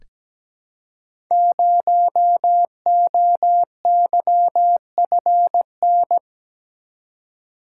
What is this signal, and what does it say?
Morse code, heard as a keyed tone, about 17 wpm, 700 Hz. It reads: 0OYFN
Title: Code Groups - 5 Letter:Numbers - 17wpm